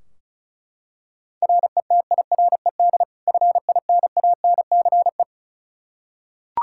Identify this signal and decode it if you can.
Morse code, heard as a keyed tone, about 36 wpm, 700 Hz. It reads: RETIRED FINANCE
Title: K1USN SST QSO Practice 35wpm